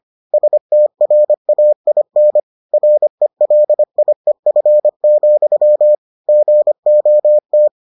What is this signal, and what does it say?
Morse code, heard as a keyed tone, about 25 wpm, 600 Hz. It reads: STRAIN RELIEF, GOT